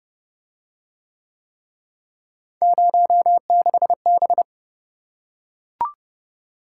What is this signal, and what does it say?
Morse code, heard as a keyed tone, about 28 wpm, 700 Hz. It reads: 06B